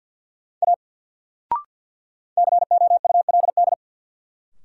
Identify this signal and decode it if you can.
Morse code, heard as a keyed tone, about 53 wpm, 700 Hz. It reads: A COULD
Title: Sentences from Top 100 Words - New Word Drill - 50wpm